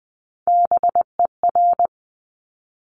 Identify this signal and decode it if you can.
Morse code, heard as a keyed tone, about 20 wpm, 700 Hz. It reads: BER
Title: Top Three Letter Combinations in Words 20wpm